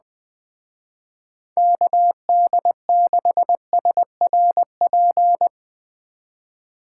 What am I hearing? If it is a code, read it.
Morse code, heard as a keyed tone, about 20 wpm, 700 Hz. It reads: KD6SRP
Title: Speed Racing - USA 2x3 Call Signs 17wpm